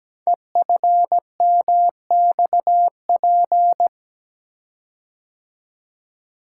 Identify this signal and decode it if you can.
Morse code, heard as a keyed tone, about 17 wpm, 700 Hz. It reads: EFMXP
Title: Code Groups - 5 Letters - 17wpm